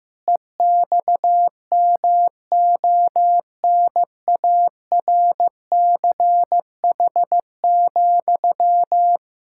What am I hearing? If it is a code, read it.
Morse code, heard as a keyed tone, about 16 wpm, 700 Hz. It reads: EXMONARCH,
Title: Pangram Challenge v2 - 15wpm